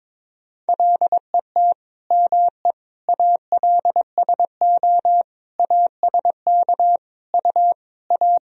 Morse code, heard as a keyed tone, about 22 wpm, 700 Hz. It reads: LET ME ALSO ASK U A
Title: Sentences from Top 400 Words and Abbreviations 22wpm